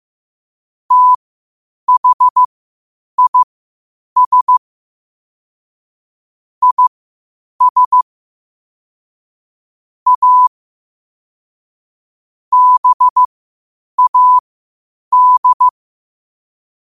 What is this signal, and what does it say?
Morse code, heard as a keyed tone, about 15 wpm, spaced as for 8 wpm, 1000 Hz. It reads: THIS IS A BAD
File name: qotdfiles / 8wpm